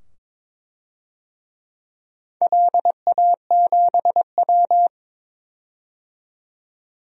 Morse code, heard as a keyed tone, about 22 wpm, 700 Hz. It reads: LA7W
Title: DXCC-100 Call Signs - 4 Characters - 22wpm